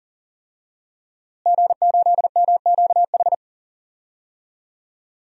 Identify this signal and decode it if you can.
Morse code, heard as a keyed tone, about 40 wpm, 700 Hz. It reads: G8MQH
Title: DXCC-100 Call Signs - 5 Characters - Repeated 3x - 40wpm